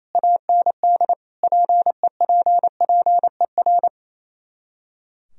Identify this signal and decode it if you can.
Morse code, heard as a keyed tone, about 26 wpm, 700 Hz. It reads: AND PEPPER
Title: Speed-Racing - Sets of 5 Words 28wpm